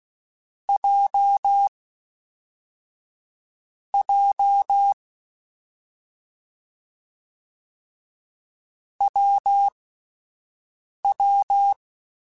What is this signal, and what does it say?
Morse code, heard as a keyed tone, about 16 wpm, spaced as for 5 wpm, 800 Hz. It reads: J J WW
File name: Difficult Combos / Difficult Combos-mp3s / WPJ WPJ1 @5